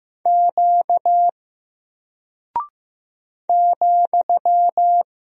Morse code, heard as a keyed tone, about 15 wpm, 700 Hz. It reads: Q ,
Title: Single Character (Letter:Number:Punctuation:Prosign) Alt 15wpm